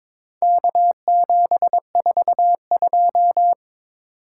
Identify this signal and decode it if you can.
Morse code, heard as a keyed tone, about 22 wpm, 700 Hz. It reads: K742
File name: practice / 22wpm / Four Letter-Number Combinations - 22wpm